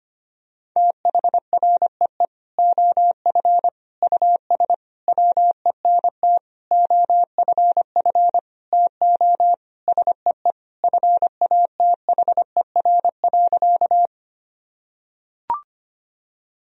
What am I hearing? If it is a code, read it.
Morse code, heard as a keyed tone, about 25 wpm, 700 Hz. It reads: THREE OF US WENT OFF TO SEE FATHER.
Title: Sentences from Top 1000 Words - Review Part 1 - 25wpm